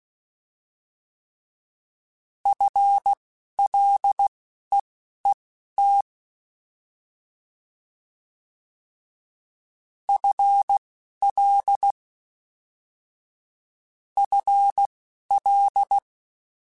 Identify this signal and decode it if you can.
Morse code, heard as a keyed tone, about 16 wpm, spaced as for 12 wpm, 800 Hz. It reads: FLEET FL FL
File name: Combinations / Combinations-mp3s / Combinations@10